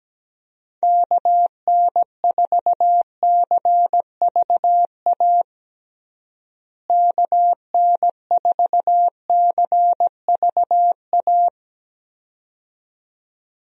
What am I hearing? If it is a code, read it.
Morse code, heard as a keyed tone, about 17 wpm, 700 Hz. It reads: KN4CVA KN4CVA
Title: USA Call Signs - 6 Characters - Repeated 2x - 17wpm